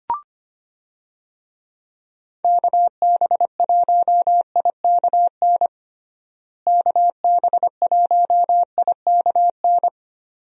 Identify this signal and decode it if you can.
Morse code, heard as a keyed tone, about 25 wpm, 700 Hz. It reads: KB1IKN KB1IKN
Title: USA Call Signs - 6 Characters - Repeated 2x - 25wpm